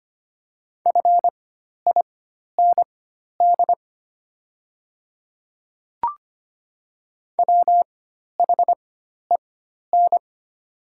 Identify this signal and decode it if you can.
Morse code, heard as a keyed tone, about 25 wpm, spaced as for 12 wpm, 700 Hz. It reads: FIND WHEN
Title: 4x Character Spacing - Top 100 Words - 25wpm